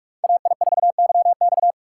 Morse code, heard as a keyed tone, about 42 wpm, 700 Hz. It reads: AI4YX